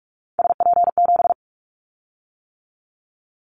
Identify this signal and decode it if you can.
Morse code, heard as a keyed tone, about 46 wpm, 700 Hz. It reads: SP7
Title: Callsign Prefixes for Top 25 DXCC - 45wpm